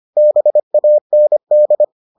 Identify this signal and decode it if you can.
Morse code, heard as a keyed tone, about 25 wpm, 600 Hz. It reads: BAND